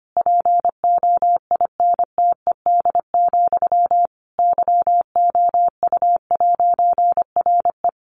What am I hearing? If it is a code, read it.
Morse code, heard as a keyed tone, about 25 wpm, 700 Hz. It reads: POINTED, YOU'RE
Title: A Sound Of Thunder 25wpm